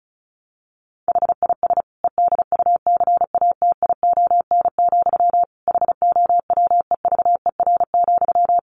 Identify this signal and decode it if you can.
Morse code, heard as a keyed tone, about 34 wpm, 700 Hz. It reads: HIS EDUCATION, HOWEVER,
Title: Sentences from Top 1400 Words 35wpm